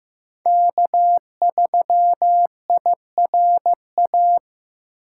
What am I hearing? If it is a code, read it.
Morse code, heard as a keyed tone, about 15 wpm, 700 Hz. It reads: K3IRA
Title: USA Call Signs - 5 Characters - 2x Character Spacing - 15wpm